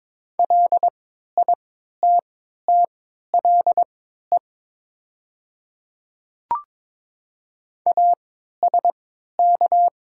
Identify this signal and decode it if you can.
Morse code, heard as a keyed tone, about 22 wpm, spaced as for 12 wpm, 700 Hz. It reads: LITTLE ASK
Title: 3x Character Spacing - Top 200 Words - 22wpm